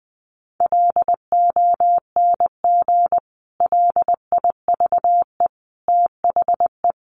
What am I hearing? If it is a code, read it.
Morse code, heard as a keyed tone, about 20 wpm, 700 Hz. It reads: LONG LIVE THE
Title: Sentences from Top 1500 Words 20wpm